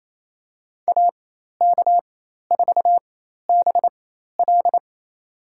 Morse code, heard as a keyed tone, about 28 wpm, spaced as for 13 wpm, 700 Hz. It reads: AK4BL
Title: USA Call Signs - 5 Characters - 4x Character Spacing - 28wpm